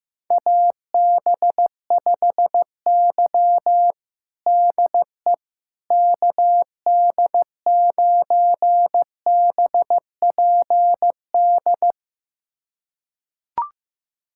Morse code, heard as a keyed tone, about 16 wpm, 700 Hz. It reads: AB5Y DE KD9BPD
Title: USA Callsign DE Callsign 15wpm